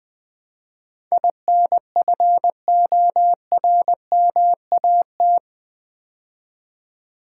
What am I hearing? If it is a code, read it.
Morse code, heard as a keyed tone, about 20 wpm, 700 Hz. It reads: INFORMAT
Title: Build A Word - 4 Letters At A Time - 11-Letter Words - 20wpm